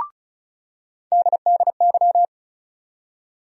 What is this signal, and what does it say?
Morse code, heard as a keyed tone, about 36 wpm, 700 Hz. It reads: DDY
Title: USA Call Sign Postfixes - 3 Letters 35wpm